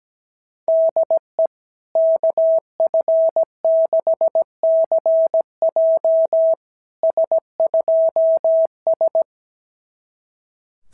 Morse code, heard as a keyed tone, about 17 wpm, 690 Hz. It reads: DE KF6CJ S2S
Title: SOTA QSO Practice 17wpm